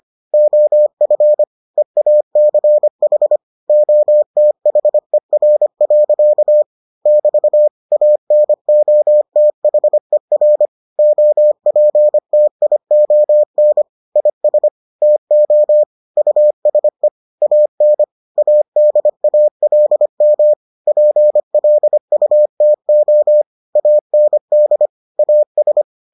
Morse code, heard as a keyed tone, about 25 wpm, 600 Hz. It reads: OF EACH OTHER. <BT> ANOTHER OPTION IS TO USE AN ADALM PLUTO AND AS